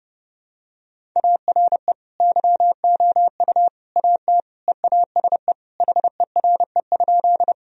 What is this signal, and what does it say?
Morse code, heard as a keyed tone, about 32 wpm, 700 Hz. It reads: ARE YOU AT EASE HERE?
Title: Sentences from Top 1000 Words - Review Part 1 - 30wpm